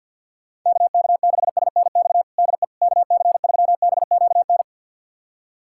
Morse code, heard as a keyed tone, about 52 wpm, 700 Hz. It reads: KK6SNX DE KC3BQN
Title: USA Callsign DE Callsign - 6 Characters - 50wpm